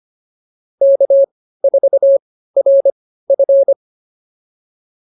Morse code, heard as a keyed tone, about 25 wpm, spaced as for 15 wpm, 550 Hz. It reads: K4RF